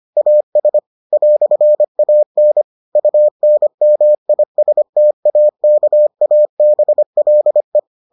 Morse code, heard as a keyed tone, about 25 wpm, 600 Hz. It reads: AS "AN UNMISTAKABLE